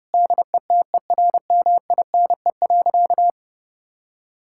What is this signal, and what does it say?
Morse code, heard as a keyed tone, about 32 wpm, 700 Hz. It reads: DETERMINE.
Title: Sentences from Top 1000 Words - Review Part 5 - 30wpm